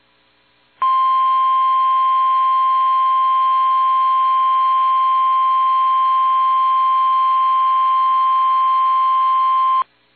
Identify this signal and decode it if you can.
Emergency Alert System (SAME) tones (the 1050 Hz attention tone). It is heard from 0.8 to 9.9 s.